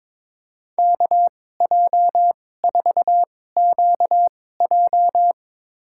Morse code, heard as a keyed tone, about 21 wpm, spaced as for 16 wpm, 700 Hz. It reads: KJ4QJ
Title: USA Call Signs - 2x Character Spacing - 22wpm